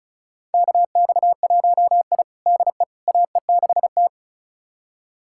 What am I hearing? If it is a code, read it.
Morse code, heard as a keyed tone, about 34 wpm, 700 Hz. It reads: KX1I DE AE6T
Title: USA Callsign DE Callsign - 4 Characters - 35wpm